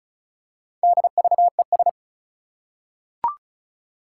Morse code, heard as a keyed tone, about 36 wpm, 700 Hz. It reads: DVES
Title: Four Challenging Letter-Number Combinations 35wpm